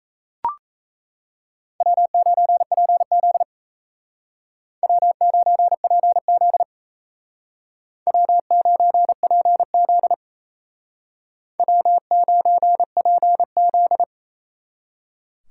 Morse code, heard as a keyed tone, about 35 wpm, 700 Hz. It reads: W9PZ W9PZ W9PZ W9PZ